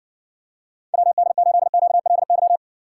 Morse code, heard as a keyed tone, about 60 wpm, 700 Hz. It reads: WD8QLY